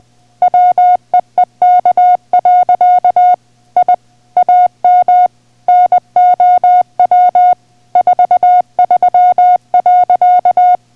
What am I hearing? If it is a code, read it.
Morse code, heard as a keyed tone, about 20 wpm, 720 Hz. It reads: WEEK. I AM NOW 43.